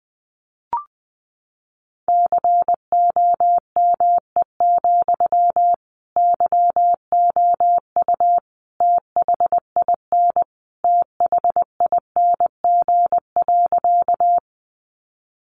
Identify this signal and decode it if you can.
Morse code, heard as a keyed tone, about 20 wpm, 700 Hz. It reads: COME, YOU THIN THING.
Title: Sentences from Top 1200 Words 20wpm